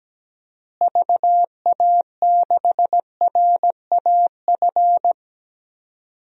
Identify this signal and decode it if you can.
Morse code, heard as a keyed tone, about 17 wpm, 700 Hz. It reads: VA6RAF
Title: DXCC-100 Call Signs - 6 Characters - Repeated 2x - 17wpm